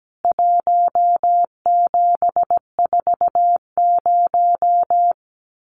Morse code, heard as a keyed tone, about 17 wpm, 700 Hz. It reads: 1740
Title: CWT Exchange 17wpm